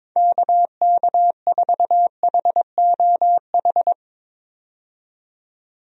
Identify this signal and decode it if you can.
Morse code, heard as a keyed tone, about 22 wpm, 700 Hz. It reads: KK4HOH
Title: USA Call Signs - 6 Characters - 22wpm